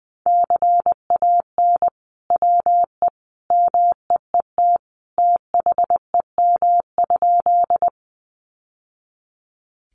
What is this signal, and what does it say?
Morse code, heard as a keyed tone, about 20 wpm, 700 Hz. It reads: CAN WE MEET THEM?
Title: Sentences from Top 1100 Words 20wpm